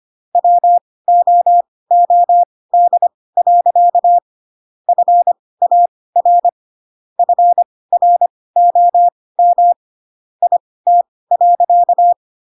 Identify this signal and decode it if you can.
Morse code, heard as a keyed tone, about 25 wpm, spaced as for 18 wpm, 700 Hz. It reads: WOOD. FAR FROM IT.
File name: Pinocchio / Pinocchio ch1 12 25 ik1hge